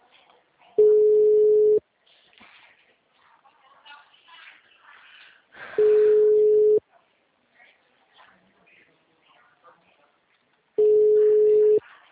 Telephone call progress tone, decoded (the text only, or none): ringback tone